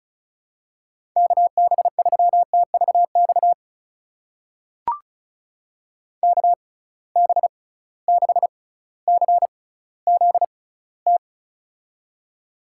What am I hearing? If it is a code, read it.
Morse code, heard as a keyed tone, about 36 wpm, spaced as for 18 wpm, 700 Hz. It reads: KB3TVX KB6CZT
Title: USA Call Signs - 6 Characters - 6x Character Spacing - 35wpm